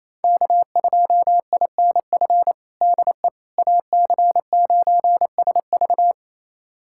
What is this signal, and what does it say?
Morse code, heard as a keyed tone, about 28 wpm, 700 Hz. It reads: K2INF DE AC9SV
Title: USA Callsign DE Callsign - 5 Characters - 28wpm